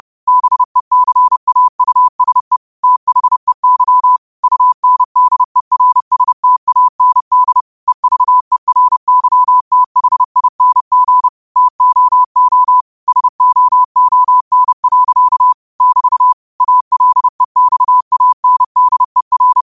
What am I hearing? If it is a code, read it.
Morse code, heard as a keyed tone, about 30 wpm, 1000 Hz. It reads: BECAUSE THEY UNDERSTAND EVERYTHING TOO SOON. <BT> ALEXANDER
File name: qotdfiles / 30wpm